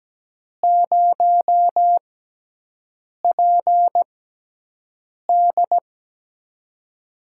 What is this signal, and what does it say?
Morse code, heard as a keyed tone, about 17 wpm, spaced as for 6 wpm, 700 Hz. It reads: 0PD
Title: USA Call Signs - 6x Character Spacing - 17wpm